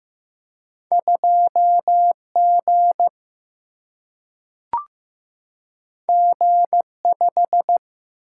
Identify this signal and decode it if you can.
Morse code, heard as a keyed tone, about 15 wpm, 700 Hz. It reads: 2 G G 5
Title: Single-Letter Single-Number 15wpm